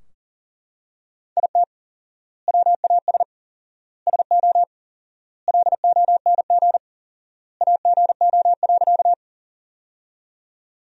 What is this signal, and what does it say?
Morse code, heard as a keyed tone, about 40 wpm, 700 Hz. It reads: IT WAS SO LONG AGO.
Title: 4x Word-Spacing - Sentences from Top 500 Words - 40wpm